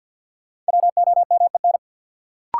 Morse code, heard as a keyed tone, about 53 wpm, 700 Hz. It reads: WOMEN
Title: Sentences from Top 1300 Words - New Word Drill Alt - 50wpm